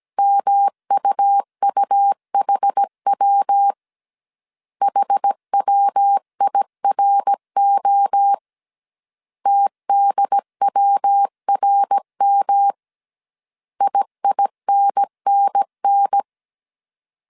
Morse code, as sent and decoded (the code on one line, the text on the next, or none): -- ..- ..- .... .-- / .... .-- .. .-. --- / - -.. .-- .-. -- / .. .. -. -. -.
MUUHW HWIRO TDWRM IINNN